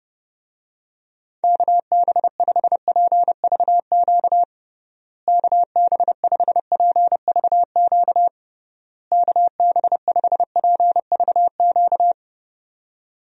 Morse code, heard as a keyed tone, about 30 wpm, 700 Hz. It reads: KB5PVQ KB5PVQ KB5PVQ
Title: USA Call Signs - 6 Characters - Repeated 3x - 30wpm